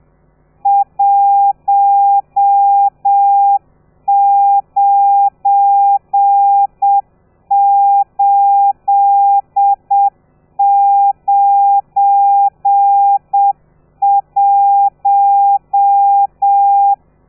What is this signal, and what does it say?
Morse code, heard as a keyed tone, about 7 wpm, 800 Hz. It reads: 19891
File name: telegrafia / i2tfj / Lez12 30